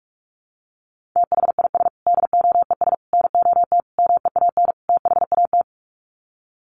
Morse code, heard as a keyed tone, about 44 wpm, 700 Hz. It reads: THIS DOES NOT MEAN THAT
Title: Sets of 5 Words Encore2 45wpm